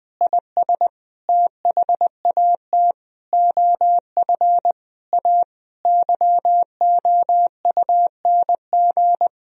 Morse code, heard as a keyed tone, about 20 wpm, 700 Hz. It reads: IS THAT OF A YOUNG
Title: Sentences from Top 500 Words and Abbreviations 20wpm